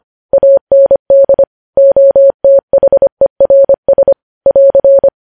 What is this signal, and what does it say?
Morse code, heard as a keyed tone, about 25 wpm, 550 Hz. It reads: AND OTHERS <AR>
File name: podcast / News headlines-1x-25x25